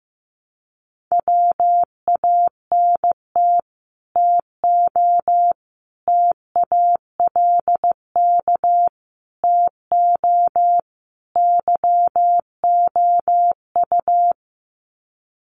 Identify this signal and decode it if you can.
Morse code, heard as a keyed tone, about 15 wpm, 700 Hz. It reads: WANT TO TALK TO YOU
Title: Sets of 5 Words Encore 15wpm